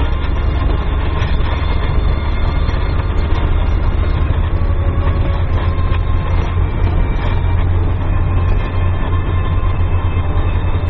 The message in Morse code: none

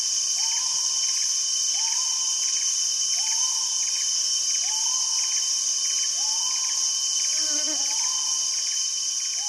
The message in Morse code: none

